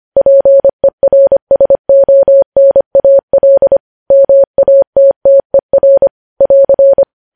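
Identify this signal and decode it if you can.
Morse code, heard as a keyed tone, about 25 wpm, 550 Hz. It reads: PERSONAL MATTER <AR>
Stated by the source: News headlines sent in Morse code at 25 wpm.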